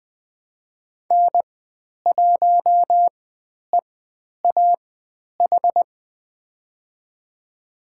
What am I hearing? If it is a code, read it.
Morse code, heard as a keyed tone, about 20 wpm, spaced as for 10 wpm, 700 Hz. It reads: N1EAH
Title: 102 Lesson 17g - Letter u - Call Signs 20wpm